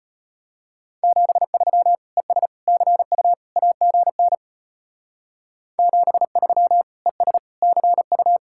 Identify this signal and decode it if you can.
Morse code, heard as a keyed tone, about 31 wpm, 700 Hz. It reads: 73 ES CU AGN 73 ES CU